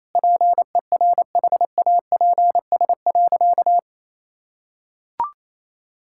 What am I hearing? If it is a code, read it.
Morse code, heard as a keyed tone, about 28 wpm, 700 Hz. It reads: PERHAPS.
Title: Sentences from Top 1600 Words 28wpm